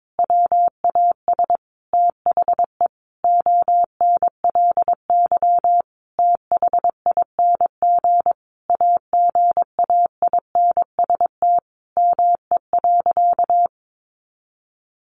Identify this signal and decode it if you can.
Morse code, heard as a keyed tone, about 22 wpm, 700 Hz. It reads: WAS THE ONLY THING AGAINST ME.
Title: Sentences from Top 1100 Words 22wpm